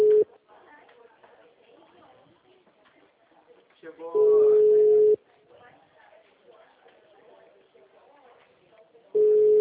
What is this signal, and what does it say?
Ringback tone (Europe, 425 Hz), with speech.